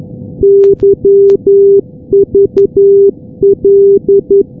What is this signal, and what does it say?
Morse code, heard as a keyed tone, about 11 wpm, 400 Hz. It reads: YVL
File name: europe / g-443-yvl-343-sdr-hu-g-south west uk- -1245z